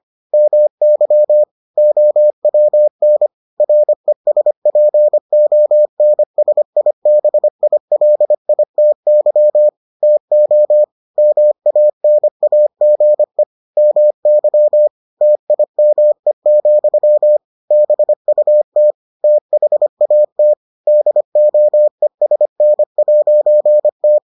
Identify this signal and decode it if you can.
Morse code, heard as a keyed tone, about 25 wpm, 600 Hz. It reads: MY OWN RESPONSIBILITY TO MANAGE MY TIME, BUT THAT DOESN'T